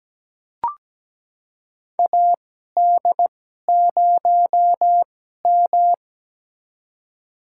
Morse code, heard as a keyed tone, about 17 wpm, spaced as for 12 wpm, 700 Hz. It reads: AD0M
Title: USA Call Signs - 4 Characters - 2x Character Spacing - 17wpm